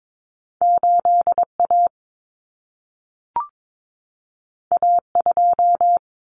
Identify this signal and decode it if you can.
Morse code, heard as a keyed tone, about 22 wpm, 700 Hz. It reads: 8A A2
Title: Common Cut Numbers - 2-Digit 22wpm